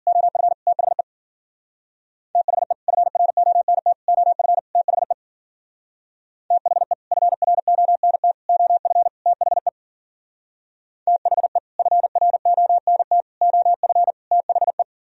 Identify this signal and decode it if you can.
Morse code, heard as a keyed tone, about 50 wpm, 700 Hz. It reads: OF THE THE FRONT OF THE THE FRONT OF THE THE FRONT OF THE